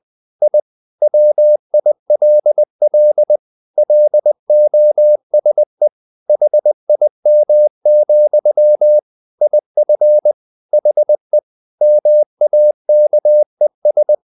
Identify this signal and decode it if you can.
Morse code, heard as a keyed tone, about 20 wpm, 600 Hz. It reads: I WILL LOSE HIM, IF HE MAKES
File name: the old man and the sea part 2 20wpm